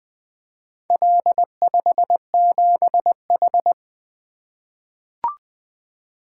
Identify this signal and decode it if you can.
Morse code, heard as a keyed tone, about 20 wpm, 700 Hz. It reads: L57H
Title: Four Letter-Number Combinations - 20wpm